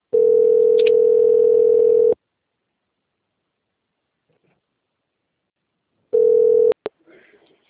Ringback tone (North America), 440 + 480 Hz.
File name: out- - - - - .6300